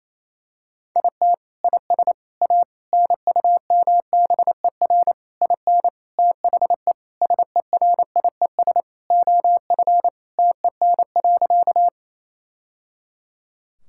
Morse code, heard as a keyed tone, about 28 wpm, 700 Hz. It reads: IT IS A NUMBER IN THE SERIES OF TEN.
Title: Sentences from Top 1000 Words - Review Part 5 - 28wpm